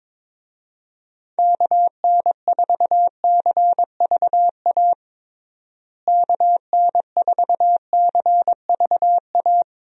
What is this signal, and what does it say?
Morse code, heard as a keyed tone, about 22 wpm, 700 Hz. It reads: KN4CVA KN4CVA
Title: USA Call Signs - 6 Characters - Repeated 2x - 22wpm